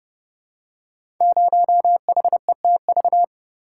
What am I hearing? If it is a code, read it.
Morse code, heard as a keyed tone, about 30 wpm, 700 Hz. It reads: 0HETV
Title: Code Groups - 5 Letter:Numbers - 30wpm